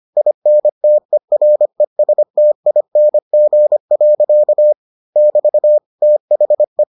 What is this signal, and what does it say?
Morse code, heard as a keyed tone, about 25 wpm, 600 Hz. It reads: INTERESTING. <BT> THE